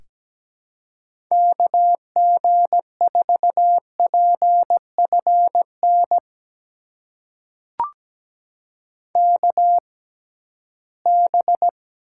Morse code, heard as a keyed tone, about 17 wpm, spaced as for 13 wpm, 700 Hz. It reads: KG4PFN KB
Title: USA Call Signs - 6 Characters - 6x Character Spacing - 17wpm